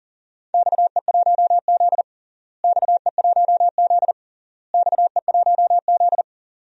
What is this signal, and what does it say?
Morse code, heard as a keyed tone, about 40 wpm, 700 Hz. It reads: XE1Z XE1Z XE1Z
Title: DXCC-100 Call Signs - 4 Characters - Repeated 3x - 40wpm